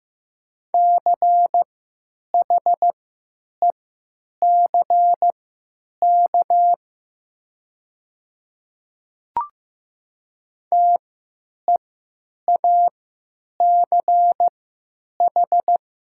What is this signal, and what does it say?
Morse code, heard as a keyed tone, about 15 wpm, spaced as for 9 wpm, 700 Hz. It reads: CHECK TEACH
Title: 3x Character Spacing - Top 500 Words - 15wpm